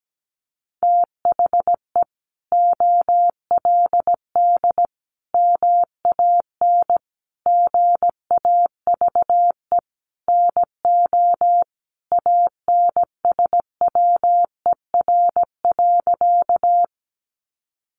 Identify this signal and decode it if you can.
Morse code, heard as a keyed tone, about 17 wpm, 700 Hz. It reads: THE OLD MAN GAVE NO ANSWER.
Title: Sentences from Top 500 Words 17wpm